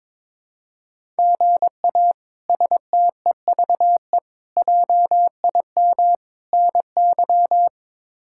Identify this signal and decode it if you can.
Morse code, heard as a keyed tone, about 21 wpm, 700 Hz. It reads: GA STEVE JIM NY
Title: K1USN SST QSO Practice - 22wpm